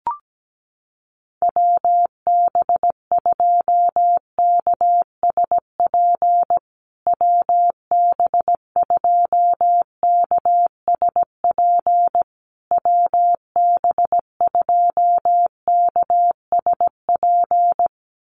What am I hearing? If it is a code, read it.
Morse code, heard as a keyed tone, about 17 wpm, 700 Hz. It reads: WB2KSP WB2KSP WB2KSP
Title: USA Call Signs Repeated 3x 17wpm